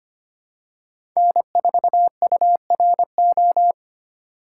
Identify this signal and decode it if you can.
Morse code, heard as a keyed tone, about 25 wpm, 700 Hz. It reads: N4URO
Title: USA Call Signs - 5 Characters - 2x Character Spacing - 25wpm